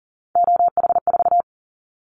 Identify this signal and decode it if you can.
Morse code, heard as a keyed tone, about 40 wpm, 700 Hz. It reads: OH4
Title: Callsign Prefixes for Top 25 DXCC - Repeated 3x - 40wpm